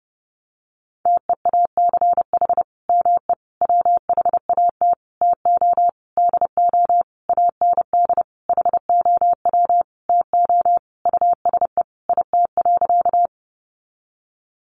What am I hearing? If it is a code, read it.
Morse code, heard as a keyed tone, about 30 wpm, 700 Hz. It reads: TEACH ME WHAT TO DO AND HOW TO USE IT.
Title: Sentences from Top 500 Words 30wpm